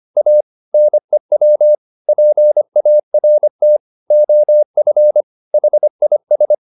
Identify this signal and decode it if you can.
Morse code, heard as a keyed tone, about 25 wpm, 600 Hz. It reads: A NEW PART OF HIS
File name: the old man and the sea part 1 25wpm